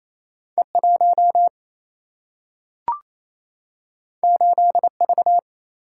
Morse code, heard as a keyed tone, about 28 wpm, 700 Hz. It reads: E1 8V
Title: Single-Letter Single-Number 28wpm